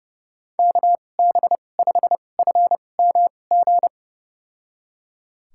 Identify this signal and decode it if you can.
Morse code, heard as a keyed tone, about 31 wpm, spaced as for 22 wpm, 700 Hz. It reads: KB5FMG
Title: USA Call Signs - 6 Characters - 2x Character Spacing - 30wpm